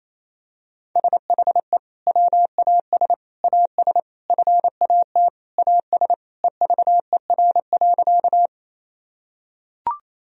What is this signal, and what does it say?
Morse code, heard as a keyed tone, about 28 wpm, 700 Hz. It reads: SHE WAS AS FAT AS EVER.
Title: Sentences from Top 1000 Words - Review Part 4 - 28wpm